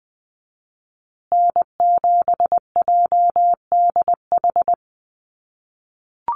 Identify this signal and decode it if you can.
Morse code, heard as a keyed tone, about 20 wpm, 700 Hz. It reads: N7JDH
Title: USA Call Signs Repeated 2x 20wpm